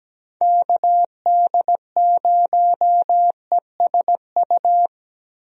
Morse code, heard as a keyed tone, about 17 wpm, 700 Hz. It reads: KD0ESU
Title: USA Call Signs - 6 Characters - Repeated 3x - 17wpm